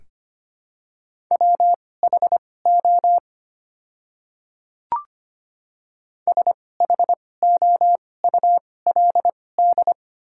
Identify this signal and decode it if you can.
Morse code, heard as a keyed tone, about 25 wpm, spaced as for 18 wpm, 700 Hz. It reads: WHO SHOULD
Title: 2x Character Spacing - Top 500 Words - 25wpm